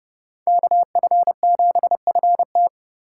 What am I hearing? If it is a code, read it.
Morse code, heard as a keyed tone, about 28 wpm, 700 Hz. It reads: KF7FT